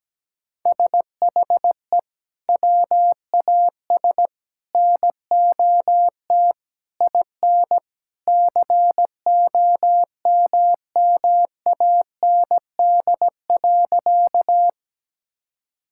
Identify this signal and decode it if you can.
Morse code, heard as a keyed tone, about 17 wpm, 700 Hz. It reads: SHE WAS NOT IN COMMAND.
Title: Sentences from Top 1600 Words 17wpm